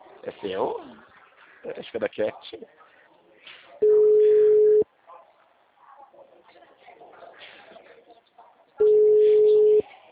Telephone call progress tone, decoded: ringback tone

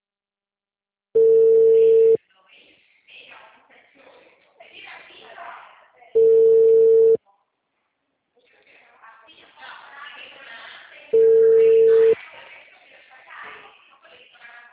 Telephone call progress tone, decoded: ringback tone

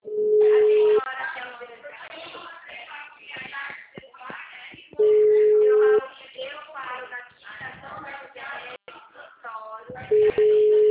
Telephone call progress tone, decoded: ringback tone

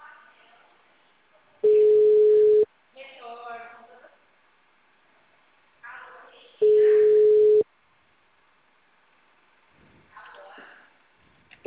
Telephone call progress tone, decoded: ringback tone

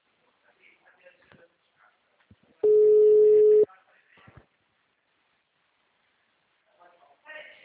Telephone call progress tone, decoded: ringback tone

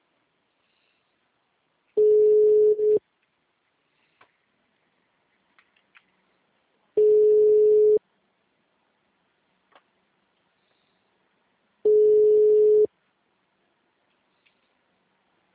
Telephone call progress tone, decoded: ringback tone

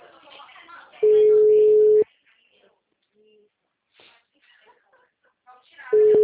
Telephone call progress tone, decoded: ringback tone

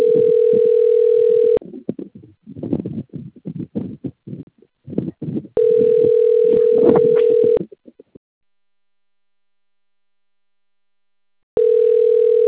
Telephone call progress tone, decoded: ringback tone

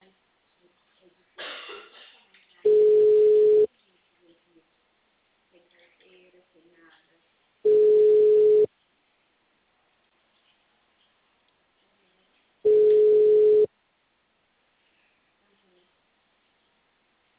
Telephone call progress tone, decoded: ringback tone

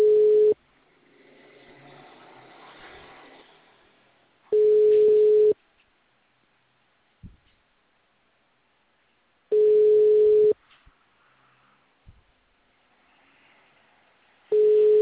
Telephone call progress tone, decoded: ringback tone

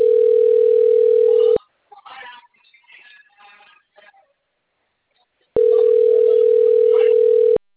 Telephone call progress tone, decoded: ringback tone

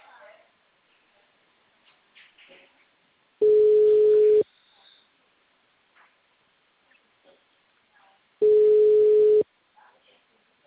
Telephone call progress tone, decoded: ringback tone

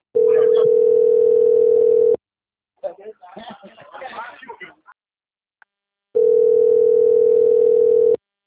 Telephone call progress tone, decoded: ringback tone

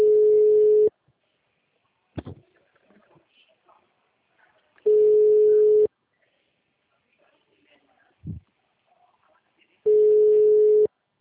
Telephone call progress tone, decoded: ringback tone

